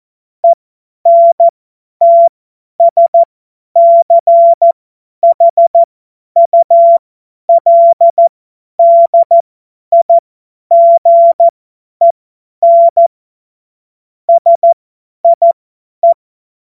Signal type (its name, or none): Morse code